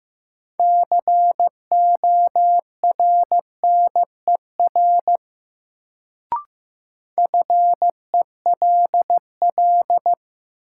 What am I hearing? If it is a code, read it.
Morse code, heard as a keyed tone, about 15 wpm, 700 Hz. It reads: CORNER FELL
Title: Sentences from Top 1600 Words - New Word Drill Alt - 15wpm